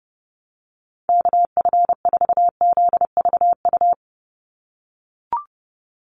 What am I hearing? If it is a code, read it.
Morse code, heard as a keyed tone, about 30 wpm, 700 Hz. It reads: KF4ZVU
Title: USA Call Signs 30wpm